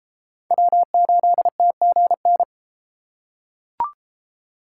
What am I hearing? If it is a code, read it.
Morse code, heard as a keyed tone, about 32 wpm, 700 Hz. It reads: W8TGN